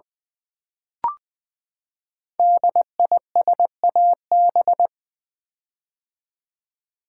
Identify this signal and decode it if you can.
Morse code, heard as a keyed tone, about 20 wpm, 700 Hz. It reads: DISAB
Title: Build A Word - 5 Letters At A Time - 10-Letter Words - 20wpm